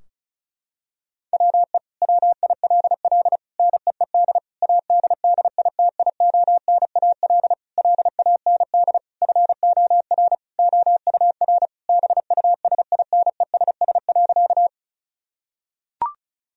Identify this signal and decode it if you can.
Morse code, heard as a keyed tone, about 36 wpm, 700 Hz. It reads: WE WILL NEED ADDITIONAL LAND FOR OUR BUSINESS.
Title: Sentences from Top 1000 Words - Review Part 4 - 35wpm